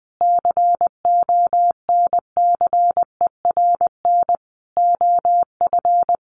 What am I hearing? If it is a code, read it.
Morse code, heard as a keyed tone, about 20 wpm, 700 Hz. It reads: CONCERN OF
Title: Sentences from Top 1300 Words 20wpm